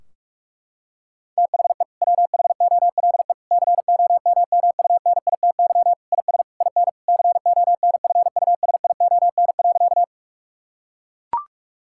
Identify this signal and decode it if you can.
Morse code, heard as a keyed tone, about 41 wpm, 700 Hz. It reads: THE WHOLE COMMUNITY IS IN CONFUSION.
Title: Sentences from Top 1600 Words 45wpm